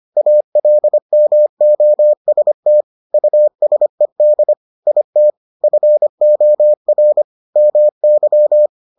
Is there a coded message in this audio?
Yes: Morse code from 0.2 to 8.7 s, heard as a keyed tone, about 25 wpm, 600 Hz. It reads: ALMOST USED IT FOR MY